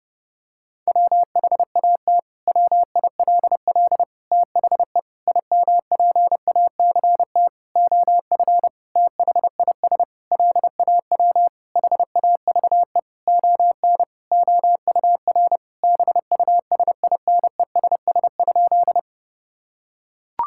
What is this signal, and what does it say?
Morse code, heard as a keyed tone, about 30 wpm, 700 Hz. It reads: WHAT WILL THE IMPACT OF THIS LAW HAVE ON OUR BUSINESS?
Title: Sentences from Top 1000 Words - Review Part 6 - 30wpm